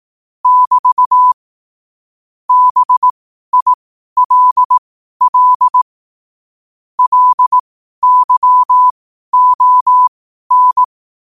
Morse code, heard as a keyed tone, about 18 wpm, spaced as for 12 wpm, 1000 Hz. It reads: <BT> BILL LYON
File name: qotdfiles / 12wpm